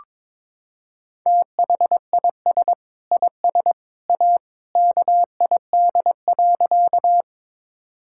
Morse code, heard as a keyed tone, about 22 wpm, 700 Hz. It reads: THIS IS A KID.
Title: Sentences from Top 1600 Words 22wpm